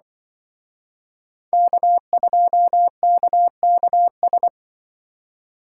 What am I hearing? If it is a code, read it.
Morse code, heard as a keyed tone, about 25 wpm, 700 Hz. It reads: K2KKS
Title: Speed Racing - USA 1x3 Call Signs 20wpm